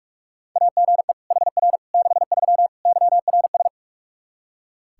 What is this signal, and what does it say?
Morse code, heard as a keyed tone, about 41 wpm, 700 Hz. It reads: AGE HR 63 YRS
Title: Practice QSO - Exchange 3 - More Info - 45wpm